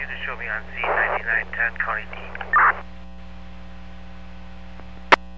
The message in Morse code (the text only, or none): none